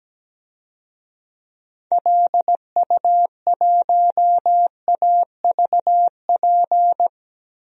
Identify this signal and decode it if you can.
Morse code, heard as a keyed tone, about 17 wpm, 700 Hz. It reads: LU1AVP
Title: DXCC-100 Call Signs - 6 Characters - Repeated 3x - 17wpm